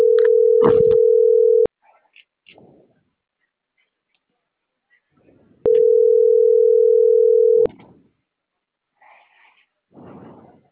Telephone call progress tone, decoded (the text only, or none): ringback tone